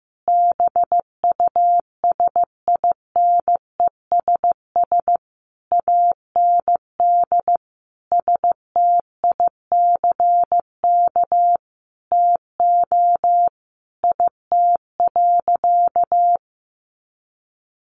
Morse code, heard as a keyed tone, about 15 wpm, 700 Hz. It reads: BUSINESS AND STICK TO IT.
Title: Sentences from Top 1200 Words 15wpm